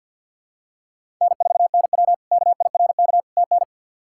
Morse code, heard as a keyed tone, about 52 wpm, 700 Hz. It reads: N4NW KIRK TN
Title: K1USN SST QSO Practice - 50wpm